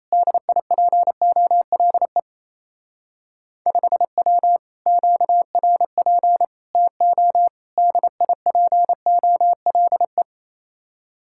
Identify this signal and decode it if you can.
Morse code, heard as a keyed tone, about 34 wpm, 700 Hz. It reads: DIPOLE 5W QRP TO DIPOLE
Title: Speed-Racing Common QSO Phrases - 28wpm